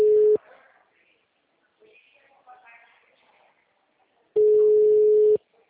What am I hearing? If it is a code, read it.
Ringback tone (Europe, 425 Hz), with speech.
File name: out- -2013- - - .74608